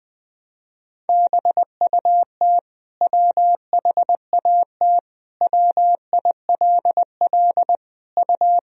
Morse code, heard as a keyed tone, about 20 wpm, 700 Hz. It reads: BUT WHAT WILL U
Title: Sentences from Top 100 Words and Abbreviations 20wpm